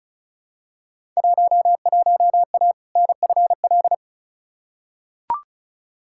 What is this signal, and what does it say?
Morse code, heard as a keyed tone, about 36 wpm, 700 Hz. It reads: 11A NFL
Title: Field Day Exchanges Encore - 35wpm